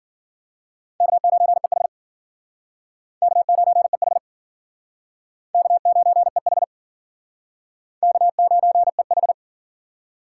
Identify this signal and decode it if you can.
Morse code, heard as a keyed tone, about 52 wpm, 700 Hz. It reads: K9EH K9EH K9EH K9EH